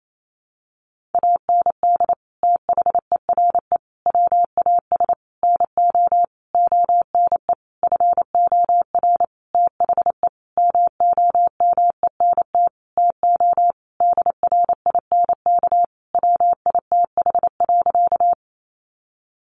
Morse code, heard as a keyed tone, about 28 wpm, 700 Hz. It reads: AND THERE WAS NO ONE FOR THE MOMENT TO DRINK WITH.
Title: Sentences from Top 1500 Words 28wpm